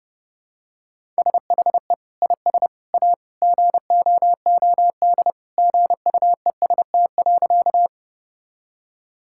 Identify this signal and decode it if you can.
Morse code, heard as a keyed tone, about 30 wpm, 700 Hz. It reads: SHE IS A GOOD GUEST.
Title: Sentences from Top 1600 Words 30wpm